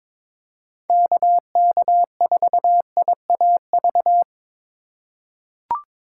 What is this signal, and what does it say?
Morse code, heard as a keyed tone, about 22 wpm, 700 Hz. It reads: KK4IAV
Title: USA Call Signs - 6 Characters - 2x Character Spacing - 22wpm